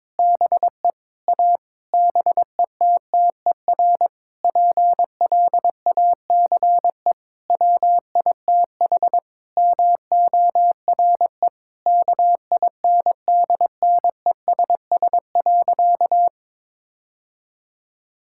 Morse code, heard as a keyed tone, about 22 wpm, 700 Hz. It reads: BE A BETTER PLACE WITH MORE KINDNESS.
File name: Sentences with 7-Word Phrases - 22wpm